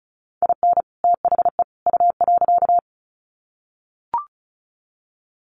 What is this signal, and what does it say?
Morse code, heard as a keyed tone, about 34 wpm, 700 Hz. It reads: IN THE U.
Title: Sets of 3 Words 35wpm